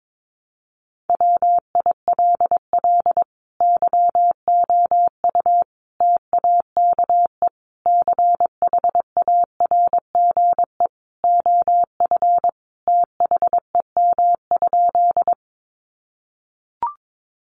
Morse code, heard as a keyed tone, about 22 wpm, 700 Hz. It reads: WILL YOU TAKE CHARGE OF THEM?
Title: Sentences from Top 900 Words 22wpm